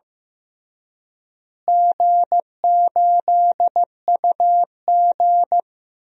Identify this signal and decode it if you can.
Morse code, heard as a keyed tone, about 15 wpm, 700 Hz. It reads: G8UG
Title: DXCC-100 Call Signs - 4 Characters - Repeated 2x - 15wpm